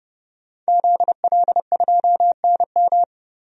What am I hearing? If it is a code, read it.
Morse code, heard as a keyed tone, about 30 wpm, 700 Hz. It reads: ZL2NM
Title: DXCC-100 Call Signs - 5 Characters - 30wpm